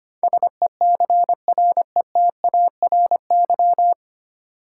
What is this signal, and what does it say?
Morse code, heard as a keyed tone, about 25 wpm, 700 Hz. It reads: SECRETARY
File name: practice / 25wpm / 9 Letter Words - First 3 Letters Given - 25wpm